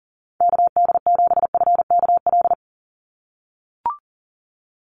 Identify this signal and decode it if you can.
Morse code, heard as a keyed tone, about 37 wpm, 700 Hz. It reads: KD7FKL
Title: USA Call Signs 40wpm